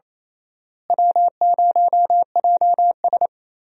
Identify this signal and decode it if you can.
Morse code, heard as a keyed tone, about 28 wpm, 700 Hz. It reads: W0JS
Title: USA Call Signs - 4 Characters - Repeated 3x - 28wpm